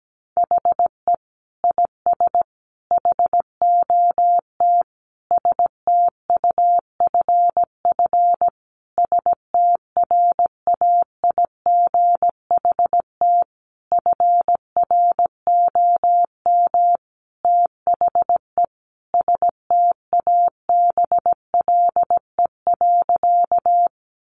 Morse code, heard as a keyed tone, about 17 wpm, 700 Hz. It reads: HE IS HOT STUFF STRAIGHT FROM THE STABLE.
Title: Sentences from Top 1500 Words 17wpm